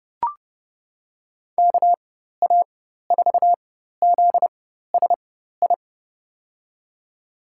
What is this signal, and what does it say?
Morse code, heard as a keyed tone, about 30 wpm, spaced as for 14 wpm, 700 Hz. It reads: KA4ZSI
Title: USA Call Signs - 4x Character Spacing - 30wpm